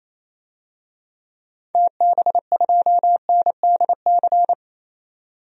Morse code, heard as a keyed tone, about 28 wpm, 700 Hz. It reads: TB2NDC
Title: DXCC-100 Call Signs - 6 Characters - 28wpm